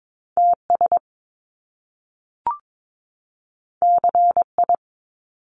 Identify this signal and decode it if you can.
Morse code, heard as a keyed tone, about 22 wpm, 700 Hz. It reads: TS CI
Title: Top Two Letter Combinations in Words 22wpm